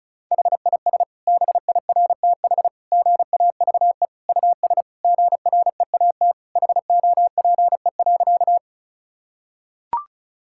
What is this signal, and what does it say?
Morse code, heard as a keyed tone, about 36 wpm, 700 Hz. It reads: HIS BIRTH GAVE US GREAT HOPE.
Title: Sentences from Top 1000 Words - Review Part 5 - 35wpm